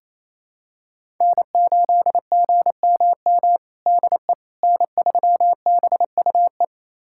Morse code, heard as a keyed tone, about 28 wpm, 700 Hz. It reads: N8GMM DE N3BUE
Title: USA Callsign DE Callsign - 5 Characters - 28wpm